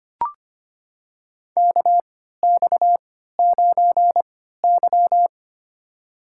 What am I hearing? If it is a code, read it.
Morse code, heard as a keyed tone, about 25 wpm, spaced as for 14 wpm, 700 Hz. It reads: KX9Y
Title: USA Call Signs - 4 Characters - 3x Character Spacing - 25wpm